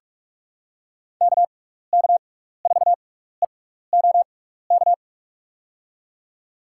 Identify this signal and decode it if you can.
Morse code, heard as a keyed tone, about 46 wpm, spaced as for 16 wpm, 700 Hz. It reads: KK4EOK
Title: USA Call Signs - 6x Character Spacing - 45wpm